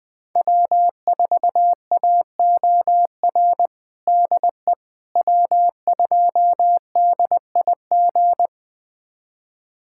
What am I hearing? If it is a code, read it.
Morse code, heard as a keyed tone, about 20 wpm, 700 Hz. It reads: W4AOR DE W2DIG
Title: USA Callsign DE Callsign - 5 Characters - 20wpm